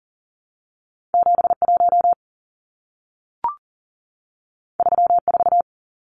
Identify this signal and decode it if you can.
Morse code, heard as a keyed tone, about 40 wpm, 700 Hz. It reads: 71 34
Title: Numbers - 2-Digit 40wpm